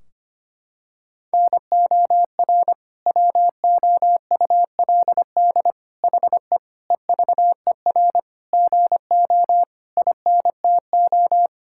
Morse code, heard as a keyed tone, about 25 wpm, 700 Hz. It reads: NOR WOULD HE EVER GO INTO
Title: Sentences from Top 1600 Words 25wpm